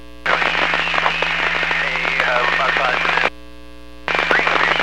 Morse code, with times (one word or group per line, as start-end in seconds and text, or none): none